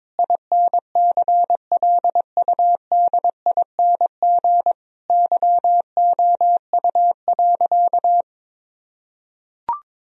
Morse code, heard as a keyed tone, about 22 wpm, 700 Hz. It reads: INCLUDING YOU.
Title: Sentences from Top 1000 Words - Review Part 4 - 22wpm